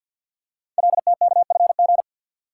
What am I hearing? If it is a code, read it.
Morse code, heard as a keyed tone, about 48 wpm, 700 Hz. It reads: LTKFG
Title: Code Groups - 5 Letters - 50wpm